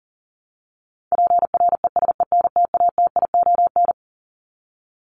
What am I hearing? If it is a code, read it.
Morse code, heard as a keyed tone, about 40 wpm, 700 Hz. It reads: PRESENTATION
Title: Top 1501-1750 Words 40wpm